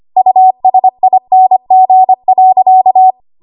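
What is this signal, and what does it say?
Morse code, heard as a keyed tone, about 25 wpm, 750 Hz. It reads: USING.
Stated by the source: ARRL 25 WPM code practice archive.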